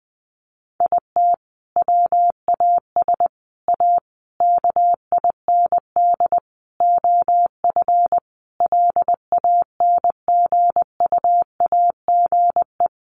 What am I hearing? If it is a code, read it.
Morse code, heard as a keyed tone, about 20 wpm, 700 Hz. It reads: IT WAS A KIND OF LANGUAGE
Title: Sentences from Top 500 Words 20wpm